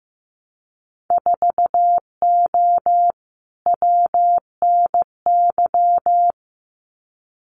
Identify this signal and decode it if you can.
Morse code, heard as a keyed tone, about 15 wpm, 700 Hz. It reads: 4O WNY
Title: Winter Field Day Exchanges Encore 15wpm